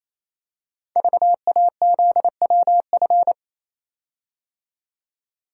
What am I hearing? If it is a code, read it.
Morse code, heard as a keyed tone, about 28 wpm, 700 Hz. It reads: VAZWF
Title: Code Groups - 5 Letter:Numbers - 28wpm